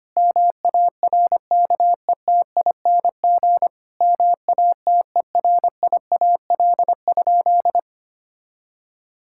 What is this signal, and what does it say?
Morse code, heard as a keyed tone, about 25 wpm, 700 Hz. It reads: MARKETING MATERIAL?
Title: Sentences from Top 1000 Words - Review Part 5 - 25wpm